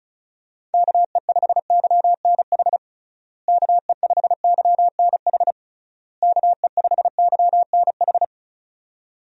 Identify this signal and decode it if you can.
Morse code, heard as a keyed tone, about 34 wpm, 700 Hz. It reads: KE5YNH KE5YNH KE5YNH
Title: USA Call Signs - 6 Characters - Repeated 3x - 35wpm